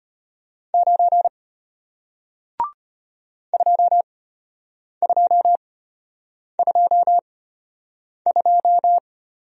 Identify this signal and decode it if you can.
Morse code, heard as a keyed tone, about 34 wpm, spaced as for 8 wpm, 700 Hz. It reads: 9 2222